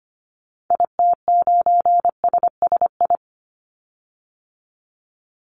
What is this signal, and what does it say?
Morse code, heard as a keyed tone, about 25 wpm, 700 Hz. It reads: IT9SSI
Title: Call Signs Repeated 2x - 25wpm